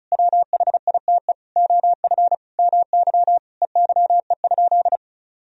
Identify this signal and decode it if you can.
Morse code, heard as a keyed tone, about 36 wpm, 700 Hz. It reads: WHITE OF MY EYE?
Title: Sentences from Top 500 Words and Abbreviations 35wpm